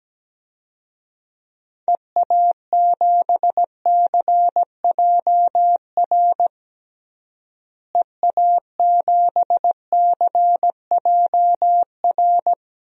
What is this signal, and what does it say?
Morse code, heard as a keyed tone, about 17 wpm, 700 Hz. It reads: EA7CJR EA7CJR
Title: DXCC-100 Call Signs - 6 Characters - Repeated 3x - 17wpm